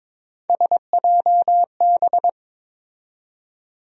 Morse code, heard as a keyed tone, about 22 wpm, 700 Hz. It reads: SJB
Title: USA Call Sign Postfixes - 3 Letters 22wpm